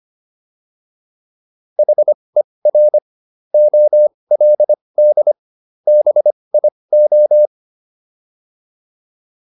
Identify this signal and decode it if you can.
Morse code, heard as a keyed tone, about 25 wpm, spaced as for 20 wpm, 600 Hz. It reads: HER OLD BIO